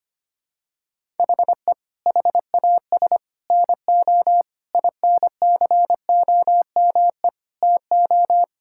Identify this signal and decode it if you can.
Morse code, heard as a keyed tone, about 25 wpm, 700 Hz. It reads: HE HAS NO INCOME TO
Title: Sentences from Top 1000 Words - Review Part 5 - 25wpm